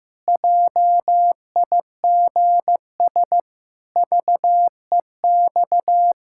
Morse code, heard as a keyed tone, about 15 wpm, 700 Hz. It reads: JIGS VEX
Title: Pangram Challenge v2 - 15wpm